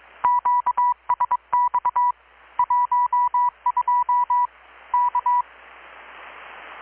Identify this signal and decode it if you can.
Morse code, heard as a keyed tone, about 22 wpm, 980 Hz. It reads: QSX 12 K
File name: blog / swl / 6-abril-2019-12695-USB-KPS